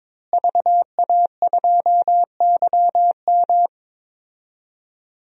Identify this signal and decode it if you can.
Morse code, heard as a keyed tone, about 22 wpm, 700 Hz. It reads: VA2YM
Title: DXCC-100 Call Signs - 5 Characters - Repeated 3x - 22wpm